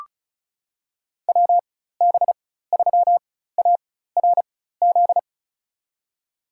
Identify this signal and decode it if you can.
Morse code, heard as a keyed tone, about 36 wpm, spaced as for 16 wpm, 700 Hz. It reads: WB3ARZ
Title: USA Call Signs - 6 Characters - 4x Character Spacing - 35wpm